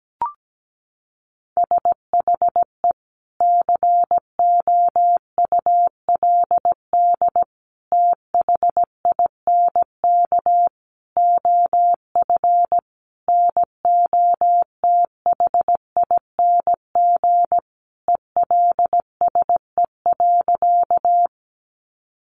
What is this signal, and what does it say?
Morse code, heard as a keyed tone, about 17 wpm, 700 Hz. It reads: SHE COULD THINK OF NOTHING ELSE.
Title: Sentences from Top 800 Words 17wpm